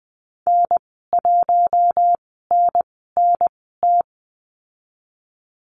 Morse code, heard as a keyed tone, about 20 wpm, spaced as for 14 wpm, 700 Hz. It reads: N1NNT
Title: 076 Lesson 13g - Letter L - Call Signs 20wpm